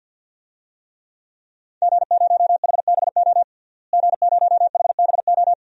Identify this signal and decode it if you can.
Morse code, heard as a keyed tone, about 52 wpm, 700 Hz. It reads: G0HBO G0HBO
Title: DXCC-100 Call Signs - 5 Characters - Repeated 3x - 50wpm